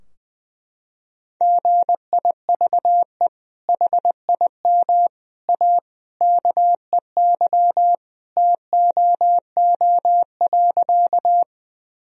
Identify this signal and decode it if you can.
Morse code, heard as a keyed tone, about 20 wpm, 700 Hz. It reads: GIVE HIM A KEY TOO.
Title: Sentences from Top 1000 Words - Review Part 4 - 20wpm